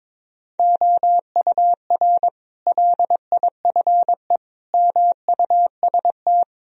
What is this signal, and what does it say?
Morse code, heard as a keyed tone, about 22 wpm, 700 Hz. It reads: OUR LIFE MUST
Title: Sentences from Top 1000 Words - Review Part 3 - 22wpm